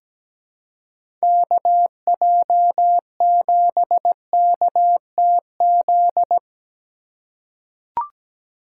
Morse code, heard as a keyed tone, about 17 wpm, 700 Hz. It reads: KJ7KTZ
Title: USA Call Signs - 6 Characters - 4x Character Spacing - 17wpm